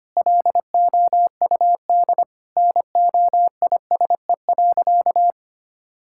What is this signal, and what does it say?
Morse code, heard as a keyed tone, about 25 wpm, 700 Hz. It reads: LOUD NOISE.
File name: practice / 25wpm / Sentences with 3-Word Phrases - 25wpm